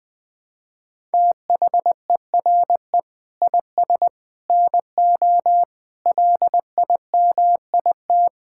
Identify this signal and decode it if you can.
Morse code, heard as a keyed tone, about 20 wpm, 700 Hz. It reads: THERE IS NO LIMIT
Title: Sets of 8 Words - 20wpm